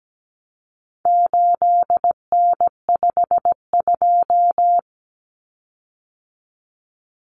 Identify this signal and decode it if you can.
Morse code, heard as a keyed tone, about 17 wpm, 700 Hz. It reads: 8N52
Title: Common Cut Numbers - 4-Digit 17wpm